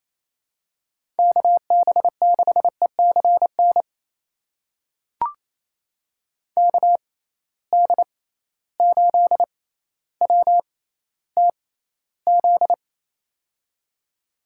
Morse code, heard as a keyed tone, about 27 wpm, spaced as for 14 wpm, 700 Hz. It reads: KB6ECN KD8WTZ
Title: USA Call Signs - 6 Characters - 6x Character Spacing - 28wpm